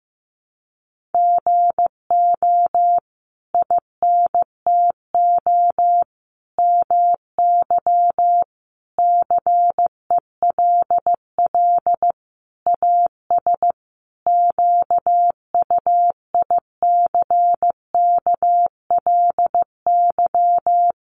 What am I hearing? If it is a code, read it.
Morse code, heard as a keyed tone, about 15 wpm, 700 Hz. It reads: GO INTO MY CELL AS QUICKLY
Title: Sentences from Top 1100 Words 15wpm